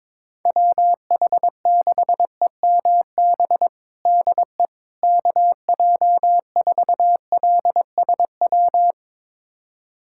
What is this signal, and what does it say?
Morse code, heard as a keyed tone, about 22 wpm, 700 Hz. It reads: WH6EMB DE KJ4LSW
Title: USA Callsign DE Callsign 22wpm